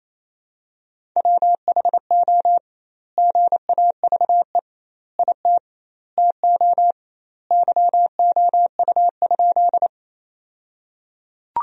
Morse code, heard as a keyed tone, about 28 wpm, 700 Hz. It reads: WHO GAVE IT TO YOU?
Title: 2x Word-Spacing - Sentences from Top 500 Words - 28wpm